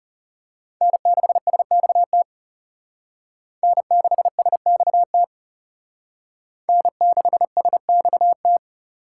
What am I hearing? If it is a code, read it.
Morse code, heard as a keyed tone, about 34 wpm, 700 Hz. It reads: N6SXT N6SXT N6SXT
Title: Speed Racing - USA Call Signs 30wpm